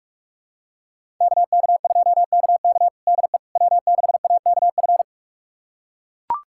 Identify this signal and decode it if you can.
Morse code, heard as a keyed tone, about 42 wpm, 700 Hz. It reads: KK2KK DE W6AKF
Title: USA Callsign DE Callsign - 5 Characters - 45wpm